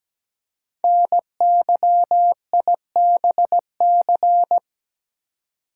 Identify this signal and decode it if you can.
Morse code, heard as a keyed tone, about 17 wpm, 700 Hz. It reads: NYIBC
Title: Code Groups - 5 Letter:Numbers - 17wpm